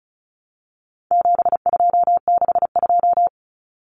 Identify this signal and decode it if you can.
Morse code, heard as a keyed tone, about 34 wpm, 700 Hz. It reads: 7262
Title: Common Cut Numbers - 4-Digit 35wpm